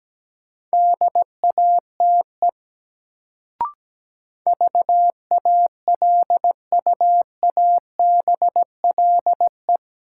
Morse code, heard as a keyed tone, about 17 wpm, 700 Hz. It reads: DATE VALUABLE
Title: Sentences from Top 1100 Words - New Word Drill Alt - 17wpm